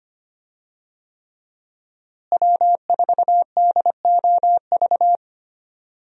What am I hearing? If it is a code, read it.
Morse code, heard as a keyed tone, about 25 wpm, 700 Hz. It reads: W4DOV
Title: USA Call Signs - 5 Characters - Repeated 3x - 25wpm